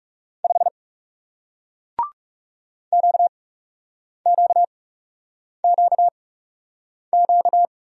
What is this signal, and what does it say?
Morse code, heard as a keyed tone, about 40 wpm, spaced as for 8 wpm, 700 Hz. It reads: 5 QQQQ